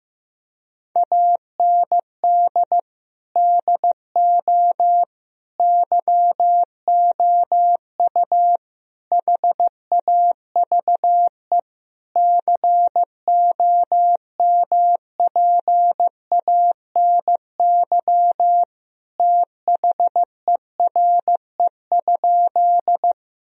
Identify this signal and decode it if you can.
Morse code, heard as a keyed tone, about 15 wpm, 700 Hz. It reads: AND DO YOU HAVE COMPANY THERE?
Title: Sentences from Top 1000 Words - Review Part 3 - 15wpm